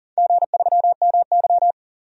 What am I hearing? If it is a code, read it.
Morse code, heard as a keyed tone, about 40 wpm, 700 Hz. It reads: G3MY
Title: DXCC-100 Call Signs - 4 Characters - 40wpm